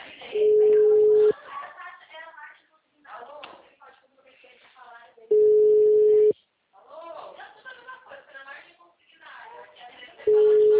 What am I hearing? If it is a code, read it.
Ringback tone (Europe, 425 Hz), with speech.
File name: out- -2033- - - .54034